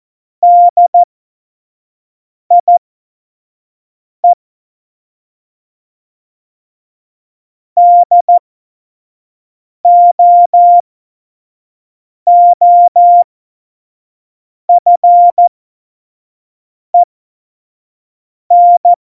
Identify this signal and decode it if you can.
Morse code, heard as a keyed tone, about 14 wpm, spaced as for 5 wpm, 700 Hz. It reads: DIE DOOFEN
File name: Morsekurs / Witze / Witz 02 Juni 2020, 25 F70